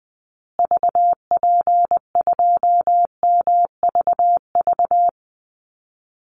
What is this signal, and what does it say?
Morse code, heard as a keyed tone, about 20 wpm, 700 Hz. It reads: VP2MVV
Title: Call Signs Repeated 2x - 20wpm